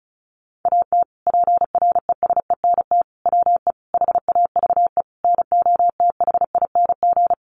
Morse code, heard as a keyed tone, about 34 wpm, 700 Hz. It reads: AT PRESENT WE HAVE NOTHING
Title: Sentences from Top 800 Words 35wpm